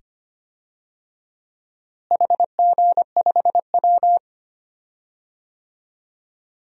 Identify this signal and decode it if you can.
Morse code, heard as a keyed tone, about 25 wpm, 700 Hz. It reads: HG5W
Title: DXCC-100 Call Signs - 4 Characters - Repeated 2x - 25wpm